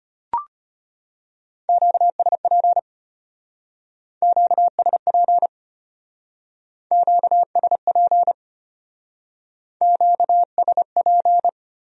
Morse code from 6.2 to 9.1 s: QSP

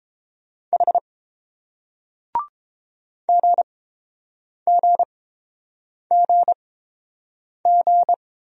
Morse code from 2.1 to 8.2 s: GGGG

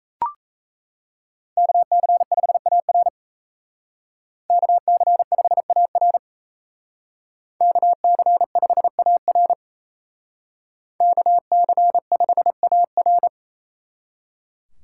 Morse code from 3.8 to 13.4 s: KC5AR KC5AR KC5AR

